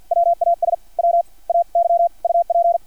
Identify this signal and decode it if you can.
Morse code, heard as a keyed tone, about 50 wpm, spaced as for 38 wpm, 680 Hz. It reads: WAS WAY UP
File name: archived / CRSnet